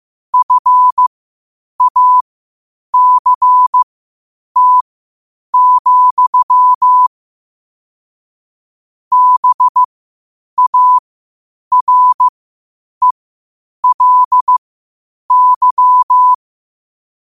Morse code, heard as a keyed tone, about 15 wpm, spaced as for 8 wpm, 1000 Hz. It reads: FACT, BARELY
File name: qotdfiles / 8wpm